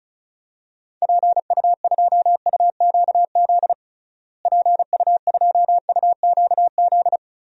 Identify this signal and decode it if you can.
Morse code, heard as a keyed tone, about 34 wpm, 700 Hz. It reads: PU2UQZ PU2UQZ
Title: DXCC-100 Call Signs - 6 Characters - Repeated 3x - 35wpm